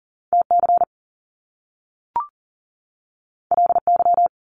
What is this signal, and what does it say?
Morse code, heard as a keyed tone, about 42 wpm, 700 Hz. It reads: TC LY
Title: Top Two Letter Combinations in Words 40wpm